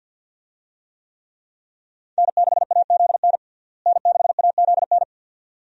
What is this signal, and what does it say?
Morse code, heard as a keyed tone, about 48 wpm, 700 Hz. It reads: N6AZN N6AZN
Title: USA Call Signs - 5 Characters - Repeated 2x - 50wpm